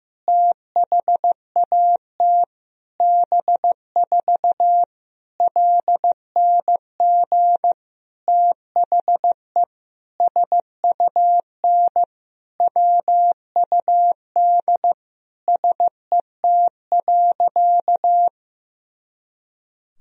Morse code, heard as a keyed tone, about 15 wpm, 700 Hz. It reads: THAT B4 LNG THE SUN WUD SET.
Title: Sentences from Top 400 Words and Abbreviations 15wpm